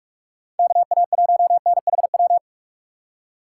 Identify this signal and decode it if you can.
Morse code, heard as a keyed tone, about 63 wpm, spaced as for 54 wpm, 700 Hz. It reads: KA1NHW
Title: Speed Racing - USA 2x3 Call Signs 45wpm